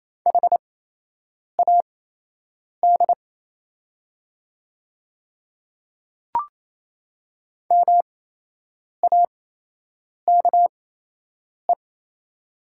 Morse code, heard as a keyed tone, about 27 wpm, spaced as for 8 wpm, 690 Hz. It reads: HAD MAKE
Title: 8x Character Spacing - Top 100 Words - 28wpm